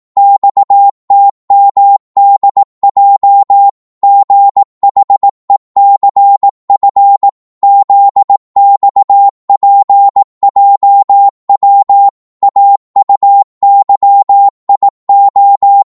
Morse code, heard as a keyed tone, about 18 wpm, 800 Hz. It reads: XTMDJ GHECF ZXPJW AUYIO
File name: Morse / ltr-18wpm